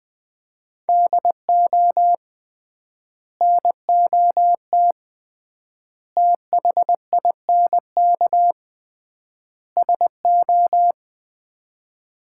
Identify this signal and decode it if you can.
Morse code, heard as a keyed tone, about 20 wpm, 700 Hz. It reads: DO NOT THINK SO
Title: 3x Word-Spacing - Sets of 4 Words - 20wpm